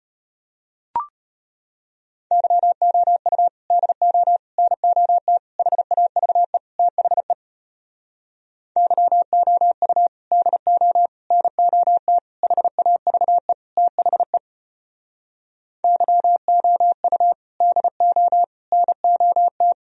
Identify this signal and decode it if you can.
Morse code, heard as a keyed tone, about 34 wpm, 700 Hz. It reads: YOU DO NOT HAVE THE YOU DO NOT HAVE THE YOU DO NOT